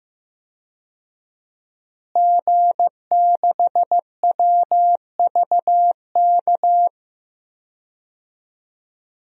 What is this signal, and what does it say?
Morse code, heard as a keyed tone, about 15 wpm, 700 Hz. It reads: G6WVK
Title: DXCC-100 Call Signs - 5 Characters - Repeated 3x - 15wpm